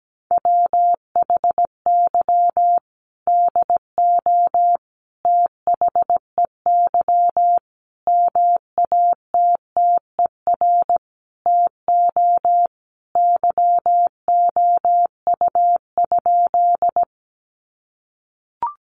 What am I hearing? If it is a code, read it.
Morse code, heard as a keyed tone, about 17 wpm, 700 Hz. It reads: WHY DO THEY MATTER TO YOU?
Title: Sentences from Top 700 Words 17wpm